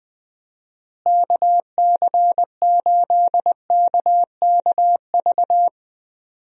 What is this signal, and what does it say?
Morse code, heard as a keyed tone, about 21 wpm, 700 Hz. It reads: KC8KKV
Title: USA Call Signs - 6 Characters - Repeated 2x - 20wpm